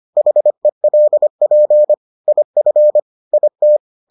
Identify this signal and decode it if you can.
Morse code, heard as a keyed tone, about 25 wpm, 600 Hz. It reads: HELP IF IT